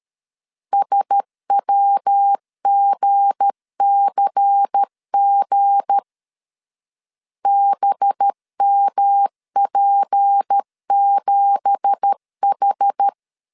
Morse code, heard as a keyed tone, about 13 wpm, 800 Hz. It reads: SWGCG BMP7H
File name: LEZIONI / 10 SIGNS / 10A SLASH 12wpm 2314